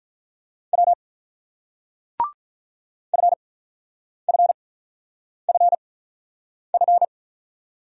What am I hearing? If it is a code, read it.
Morse code, heard as a keyed tone, about 50 wpm, spaced as for 9 wpm, 700 Hz. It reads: W FFFF